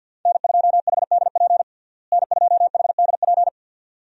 Morse code, heard as a keyed tone, about 52 wpm, 700 Hz. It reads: N2HDP N2HDP
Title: USA Call Signs - 5 Characters - Repeated 2x - 50wpm